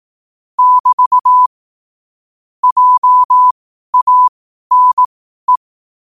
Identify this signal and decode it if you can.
Morse code, heard as a keyed tone, about 18 wpm, spaced as for 12 wpm, 1000 Hz. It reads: <BT> JANE
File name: qotdfiles / 12wpm